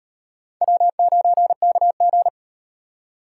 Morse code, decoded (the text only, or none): W9KG